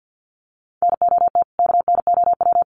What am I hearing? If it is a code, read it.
Morse code, heard as a keyed tone, about 50 wpm, 700 Hz. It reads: NOT KNOW
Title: Sets of 2 Words 50wpm